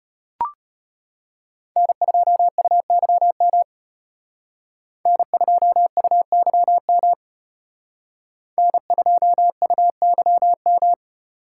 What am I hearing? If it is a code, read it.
Morse code, heard as a keyed tone, about 31 wpm, 700 Hz. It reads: N2UYM N2UYM N2UYM
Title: Speed Racing - USA 1x3 Call Signs 25wpm